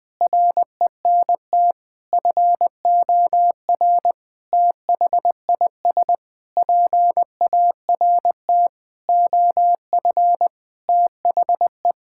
Morse code, heard as a keyed tone, about 20 wpm, 700 Hz. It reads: RENT FOR THIS PART OF THE
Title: Sentences from Top 1000 Words - Review Part 5 - 20wpm